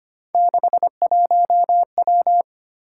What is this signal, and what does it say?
Morse code, heard as a keyed tone, about 25 wpm, 700 Hz. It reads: 61W
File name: Three Letter-Number Combinations - 25wpm